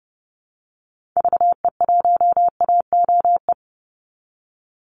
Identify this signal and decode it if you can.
Morse code, heard as a keyed tone, about 30 wpm, 700 Hz. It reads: VE1AOE
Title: Call Signs Repeated 3x - 30wpm